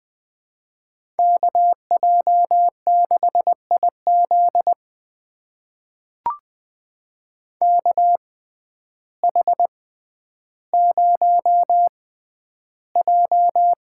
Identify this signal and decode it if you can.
Morse code, heard as a keyed tone, about 20 wpm, spaced as for 11 wpm, 700 Hz. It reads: KJ6IZ KH0J
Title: USA Call Signs - 5 Characters - 6x Character Spacing - 20wpm